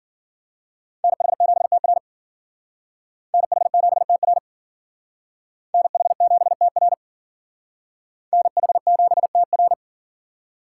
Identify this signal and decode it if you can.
Morse code, heard as a keyed tone, about 51 wpm, 700 Hz. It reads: NH7TR NH7TR NH7TR NH7TR